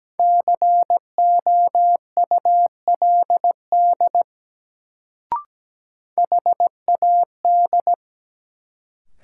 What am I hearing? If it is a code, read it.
Morse code, heard as a keyed tone, about 17 wpm, 700 Hz. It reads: COULD HAD
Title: Sets of 2 Words - New Word Drill - 17wpm